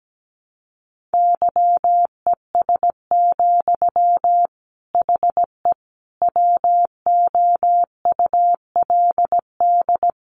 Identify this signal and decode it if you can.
Morse code, heard as a keyed tone, about 17 wpm, 700 Hz. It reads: YES, HE WOULD
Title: Sentences from Top 900 Words 17wpm